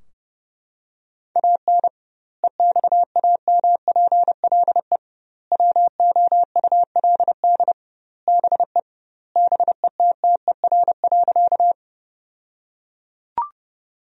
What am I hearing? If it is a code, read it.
Morse code, heard as a keyed tone, about 30 wpm, 700 Hz. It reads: AN EXAMPLE WOULD BE BETTER.
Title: 2x Word-Spacing - Sentences from Top 400 Words - 30wpm